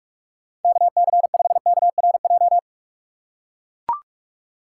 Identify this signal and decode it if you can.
Morse code, heard as a keyed tone, about 41 wpm, 700 Hz. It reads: KC5KRJ
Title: USA Call Signs - 6 Characters - 2x Character Spacing - 45wpm